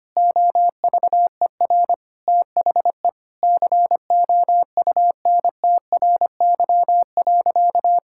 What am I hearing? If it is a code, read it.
Morse code, heard as a keyed tone, about 25 wpm, 700 Hz. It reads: OVER THE COUNTRY.
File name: practice / 25wpm / Sentences with 4-Word Phrases - 25wpm